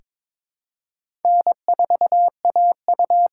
Morse code, heard as a keyed tone, about 22 wpm, 700 Hz. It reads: N4AU
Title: USA Call Signs - 5 Characters - 6x Character Spacing - 22wpm